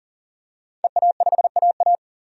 Morse code, heard as a keyed tone, about 40 wpm, 700 Hz. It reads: EA5AA
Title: DXCC-100 Call Signs - 5 Characters - 40wpm